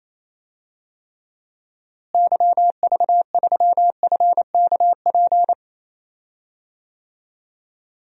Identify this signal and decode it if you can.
Morse code, heard as a keyed tone, about 28 wpm, 700 Hz. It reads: YV3FKP
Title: DXCC-100 Call Signs - 6 Characters - Repeated 2x - 28wpm